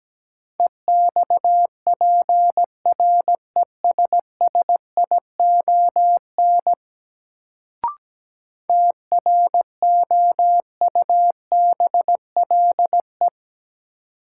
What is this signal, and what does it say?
Morse code, heard as a keyed tone, about 17 wpm, 700 Hz. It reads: EXPRESSION TROUBLE
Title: Sentences from Top 1200 Words - New Word Drill - 17wpm